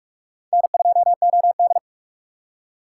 Morse code, heard as a keyed tone, about 46 wpm, 700 Hz. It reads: N2OD
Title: USA Call Signs - 4 Characters - 2x Character Spacing - 45wpm